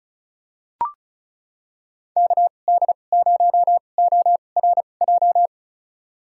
Morse code, heard as a keyed tone, about 34 wpm, spaced as for 25 wpm, 690 Hz. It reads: KD0ORJ
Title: USA Call Signs - 6 Characters - 2x Character Spacing - 35wpm